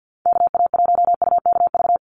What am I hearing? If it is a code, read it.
Morse code, heard as a keyed tone, about 50 wpm, 700 Hz. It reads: KA1UKV
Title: USA Call Signs 50wpm